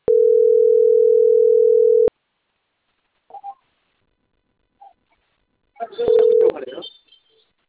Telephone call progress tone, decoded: ringback tone